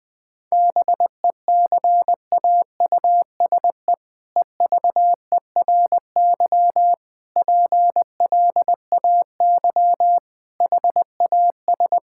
Morse code, heard as a keyed tone, about 20 wpm, 700 Hz. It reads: BECAUSE EVERY PLAY HAS
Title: All-Age Appropriate Jokes - 20wpm